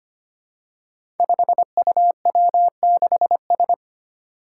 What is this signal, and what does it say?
Morse code, heard as a keyed tone, about 25 wpm, 700 Hz. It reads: 5UW6S
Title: Five Challenging Letter-Number Combinations 25wpm